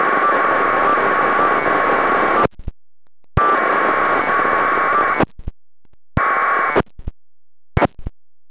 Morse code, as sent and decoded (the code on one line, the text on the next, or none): none